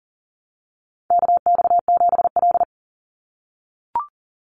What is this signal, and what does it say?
Morse code, heard as a keyed tone, about 40 wpm, 700 Hz. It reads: KX7L
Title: NA SOTA Chaser Callsigns 40wpm